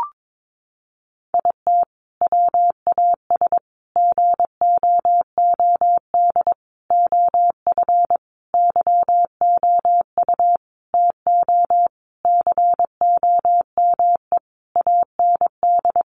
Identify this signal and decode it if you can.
Morse code, heard as a keyed tone, about 22 wpm, 700 Hz. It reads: IT WAS GOOD OF YOU TO COME AND
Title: Sentences from Top 1100 Words 22wpm